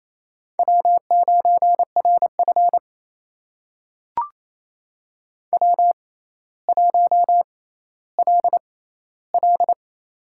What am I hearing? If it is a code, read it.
Morse code, heard as a keyed tone, about 28 wpm, spaced as for 14 wpm, 700 Hz. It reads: W9RF W1LL
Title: USA Call Signs - 4 Characters - 6x Character Spacing - 28wpm